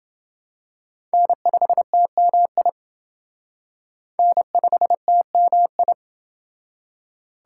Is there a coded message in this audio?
Yes: Morse code from 1.1 to 5.9 s, heard as a keyed tone, about 28 wpm, 700 Hz. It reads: N5TMI N5TMI